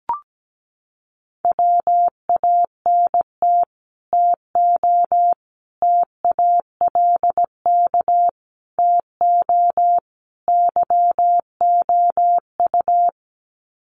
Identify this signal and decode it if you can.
Morse code, heard as a keyed tone, about 17 wpm, 700 Hz. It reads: WANT TO TALK TO YOU
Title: Sets of 5 Words Encore 17wpm